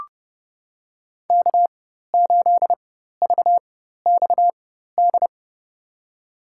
Morse code, heard as a keyed tone, about 32 wpm, spaced as for 14 wpm, 700 Hz. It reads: K8VXD
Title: USA Call Signs - 5 Characters - 4x Character Spacing - 30wpm